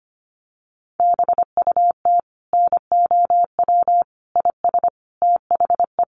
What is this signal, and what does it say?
Morse code, heard as a keyed tone, about 25 wpm, 700 Hz. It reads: BUT NOW IS THE
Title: Sentences from Top 1500 Words 25wpm